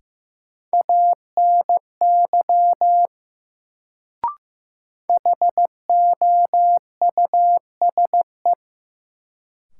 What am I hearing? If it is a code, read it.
Morse code, heard as a keyed tone, about 15 wpm, 700 Hz. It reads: ANY HOUSE
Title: Sentences from Top 200 Words - New Word Drill - 15wpm